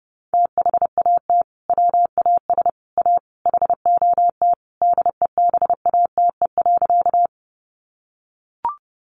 Morse code, heard as a keyed tone, about 30 wpm, 700 Hz. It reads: THAT WAS A HOT DEBATE.
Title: Sentences from Top 1100 Words 30wpm